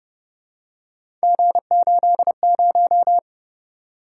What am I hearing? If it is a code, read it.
Morse code, heard as a keyed tone, about 30 wpm, 700 Hz. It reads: G80